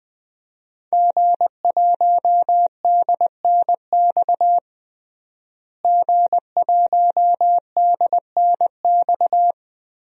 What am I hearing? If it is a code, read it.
Morse code, heard as a keyed tone, about 20 wpm, 700 Hz. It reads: G1DNX G1DNX
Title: DXCC-100 Call Signs - 5 Characters - Repeated 3x - 20wpm